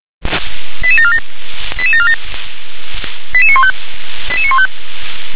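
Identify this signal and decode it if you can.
Five-tone selective call (ZVEI), heard as quick runs of five tones, tones of about 70 ms. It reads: 88747 88747 88915 88915
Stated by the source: Five-tone ZVEI selective call received by SDR.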